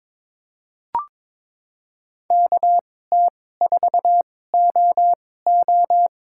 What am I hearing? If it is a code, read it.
Morse code, heard as a keyed tone, about 22 wpm, spaced as for 16 wpm, 700 Hz. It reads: KT4OO
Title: USA Call Signs - 2x Character Spacing - 22wpm